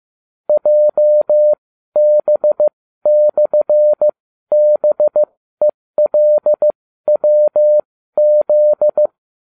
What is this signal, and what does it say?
Morse code, heard as a keyed tone, about 15 wpm, spaced as for 12 wpm, 600 Hz. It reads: JB/BELWZ